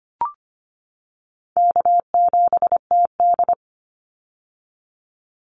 Morse code, heard as a keyed tone, about 25 wpm, 700 Hz. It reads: K7TD
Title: Easy Call Signs - 25wpm